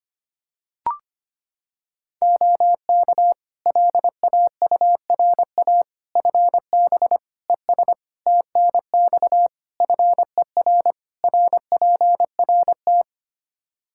Morse code, heard as a keyed tone, about 25 wpm, 700 Hz. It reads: OK LAURA FB ES TNX FER RPRT
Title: K1USN SST QSO Practice 25wpm